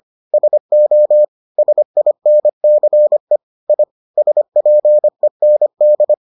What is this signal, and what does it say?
Morse code, heard as a keyed tone, about 25 wpm, 600 Hz. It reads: SO SINCE I SPEND